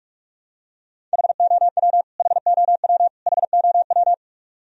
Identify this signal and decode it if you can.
Morse code, heard as a keyed tone, about 43 wpm, 700 Hz. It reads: HOW HOW HOW
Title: Top 100 Words Repeated 45wpm